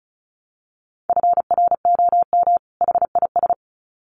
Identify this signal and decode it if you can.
Morse code, heard as a keyed tone, about 34 wpm, 700 Hz. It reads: FROM HIS
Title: Sets of 2 Words 35wpm